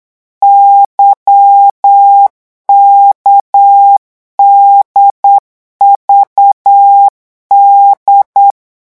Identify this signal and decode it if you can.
Morse code, heard as a keyed tone, about 8 wpm, 800 Hz. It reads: YKDVD
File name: morse mp3 / revision3-vitesse 8 mots